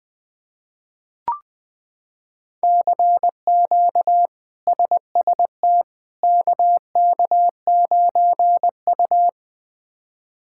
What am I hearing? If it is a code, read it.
Morse code, heard as a keyed tone, about 20 wpm, 700 Hz. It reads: CQ SST KK9U
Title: K1USN SST QSO Practice - 20wpm